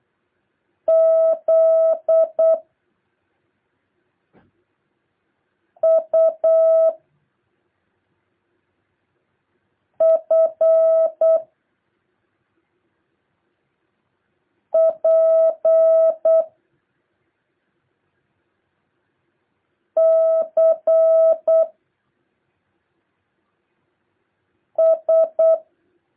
Morse code, heard as a keyed tone, about 8 wpm, 650 Hz. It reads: Z U F P C S
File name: jh8cbh / 2023cw